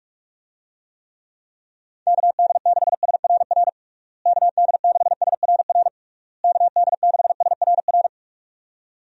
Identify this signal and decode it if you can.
Morse code, heard as a keyed tone, about 41 wpm, 700 Hz. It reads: KD6SRR KD6SRR KD6SRR
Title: USA Call Signs - 6 Characters - Repeated 3x - 45wpm